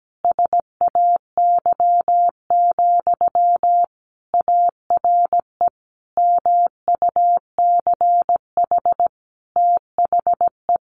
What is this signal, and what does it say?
Morse code, heard as a keyed tone, about 17 wpm, 700 Hz. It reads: SAY, ARE MUCH THE
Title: Sentences from Top 1100 Words 17wpm